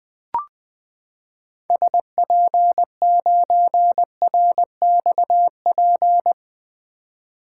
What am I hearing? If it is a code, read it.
Morse code, heard as a keyed tone, about 21 wpm, 700 Hz. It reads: SP9RXP
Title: 2023 POTA Hunter Call Signs - 20wpm